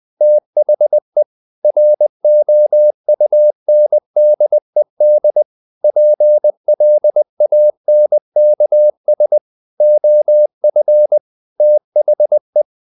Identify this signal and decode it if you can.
Morse code, heard as a keyed tone, about 20 wpm, 600 Hz. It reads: THE ROUNDED PLANKS OF THE
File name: the old man and the sea / the old man and the sea part 1 20wpm